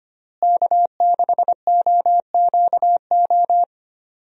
Morse code, heard as a keyed tone, about 25 wpm, 700 Hz. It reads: K6OQO
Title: USA Call Signs - 5 Characters - Repeated 3x - 25wpm